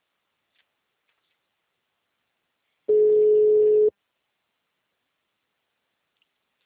Ringback tone (Europe, 425 Hz).
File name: out- -2017- - - .69771